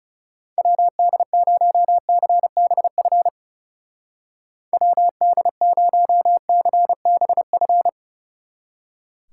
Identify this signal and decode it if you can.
Morse code, heard as a keyed tone, about 36 wpm, 700 Hz. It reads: WD0CBF WD0CBF
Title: Speed Racing - USA Call Signs 30wpm